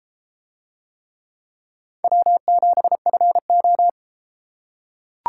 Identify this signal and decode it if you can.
Morse code, heard as a keyed tone, about 31 wpm, 700 Hz. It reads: W7FO